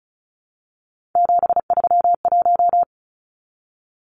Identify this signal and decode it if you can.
Morse code, heard as a keyed tone, about 34 wpm, 700 Hz. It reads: 731
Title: Numbers - 3-Digit 35wpm